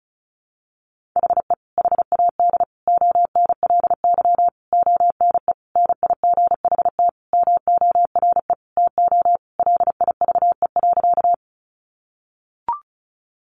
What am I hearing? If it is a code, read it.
Morse code, heard as a keyed tone, about 34 wpm, 700 Hz. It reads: HE HAD ONLY ONE NIGHT MORE TO LIVE.
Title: Sentences from Top 300 Words 35wpm